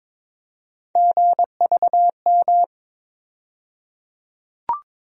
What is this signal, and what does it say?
Morse code, heard as a keyed tone, about 22 wpm, 700 Hz. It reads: GVM
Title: USA Call Sign Postfixes - 3 Letters 22wpm